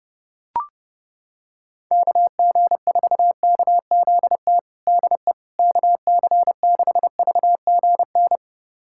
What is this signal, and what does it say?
Morse code, heard as a keyed tone, about 32 wpm, 700 Hz. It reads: KG4KZT DE KC6VGN
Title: USA Callsign DE Callsign 30wpm